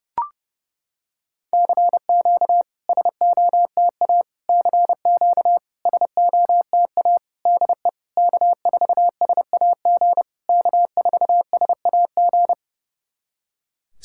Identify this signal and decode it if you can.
Morse code, heard as a keyed tone, about 28 wpm, 690 Hz. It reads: CQ SOTA CQ SOTA DE K4SAG K4SAG
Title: SOTA QSO Practice 30wpm